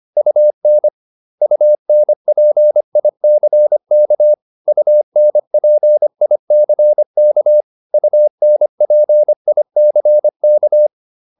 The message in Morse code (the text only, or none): UN UNPICK UNPICK UNPICK